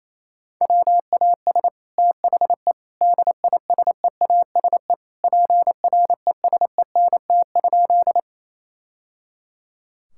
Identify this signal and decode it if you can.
Morse code, heard as a keyed tone, about 28 wpm, 700 Hz. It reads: WAS THE DISEASE PRESENT?
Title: Sentences from Top 1000 Words - Review Part 6 - 28wpm